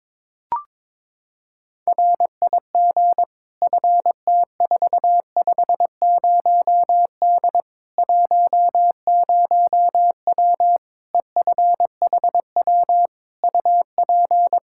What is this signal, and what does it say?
Morse code, heard as a keyed tone, about 22 wpm, 700 Hz. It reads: RIG FT450D 10W EFHW UP
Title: K1USN SST QSO Practice 22wpm